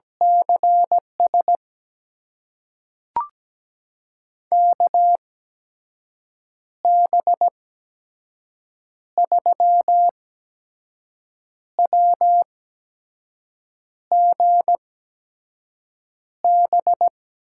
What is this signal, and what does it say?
Morse code, heard as a keyed tone, about 17 wpm, spaced as for 5 wpm, 700 Hz. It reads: CS KB3WGB
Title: USA Call Signs - 8x Character Spacing - 17wpm